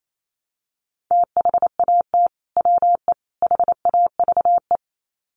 Morse code, heard as a keyed tone, about 28 wpm, 700 Hz. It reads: THAT WE HAVE
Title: Sets of 3 Words 28wpm